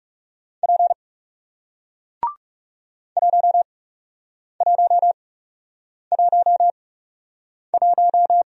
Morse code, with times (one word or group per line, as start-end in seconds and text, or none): 0.6-0.9 P
3.2-8.4 1111